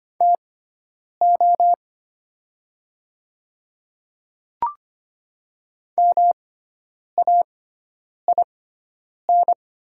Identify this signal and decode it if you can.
Morse code, heard as a keyed tone, about 25 wpm, spaced as for 9 wpm, 700 Hz. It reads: TO MAIN
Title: 6x Character Spacing - Top 300 Words - 25wpm